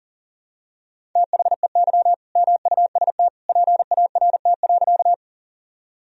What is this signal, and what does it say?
Morse code, heard as a keyed tone, about 40 wpm, 700 Hz. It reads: THEY MUST PART.
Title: Sentences from Top 1000 Words - Review Part 1 - 40wpm